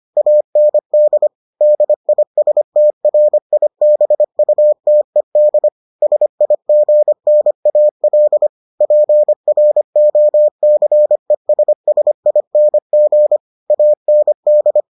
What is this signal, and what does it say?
Morse code, heard as a keyed tone, about 25 wpm, 600 Hz. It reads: AND DISTRIBUTED SIGNAL PROCESSING AND